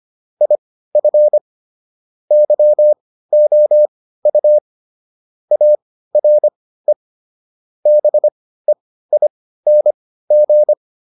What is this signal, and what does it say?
Morse code, heard as a keyed tone, about 25 wpm, spaced as for 15 wpm, 600 Hz. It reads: IF YOU ARE BEING